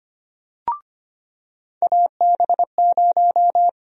Morse code, heard as a keyed tone, about 25 wpm, 700 Hz. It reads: AB0
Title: USA Call Sign Prefixes - 25wpm